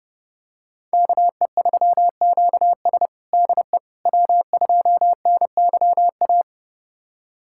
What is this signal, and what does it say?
Morse code, heard as a keyed tone, about 30 wpm, 700 Hz. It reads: KE3QS DE W2NYA
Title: USA Callsign DE Callsign - 5 Characters - 30wpm